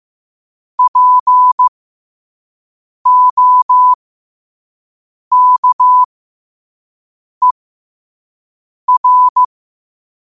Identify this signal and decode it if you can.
Morse code, heard as a keyed tone, about 15 wpm, spaced as for 5 wpm, 1000 Hz. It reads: POKER
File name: qotdfiles / 5wpm